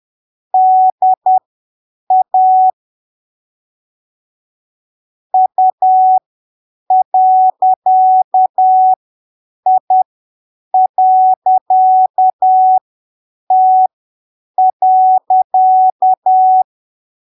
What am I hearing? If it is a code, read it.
Morse code, heard as a keyed tone, about 10 wpm, spaced as for 7 wpm, 750 Hz. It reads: DA U.I.T.
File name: arquivos / texto / texto 07 10ppm